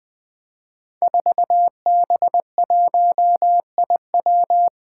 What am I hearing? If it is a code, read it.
Morse code, heard as a keyed tone, about 20 wpm, 700 Hz. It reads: 4B1IW
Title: Five Challenging Letter-Number Combinations 20wpm